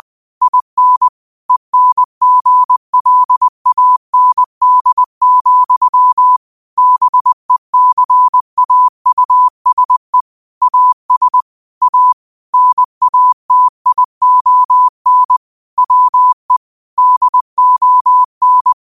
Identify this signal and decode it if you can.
Morse code, heard as a keyed tone, about 20 wpm, 1000 Hz. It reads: IN ENGLAND, BECAUSE AS A NATION WE DON
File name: qotdfiles / 20wpm